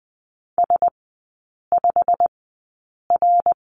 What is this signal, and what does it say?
Morse code, heard as a keyed tone, about 20 wpm, spaced as for 8 wpm, 700 Hz. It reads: S5R
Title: 100 Lesson 17e - Letter u - Three Characters 20wpm